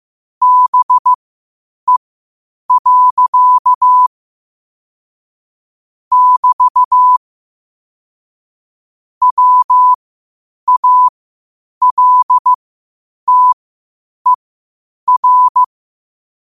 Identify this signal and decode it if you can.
Morse code, heard as a keyed tone, about 15 wpm, spaced as for 8 wpm, 1000 Hz. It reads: BE. <BT> WALTER
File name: qotdfiles / 8wpm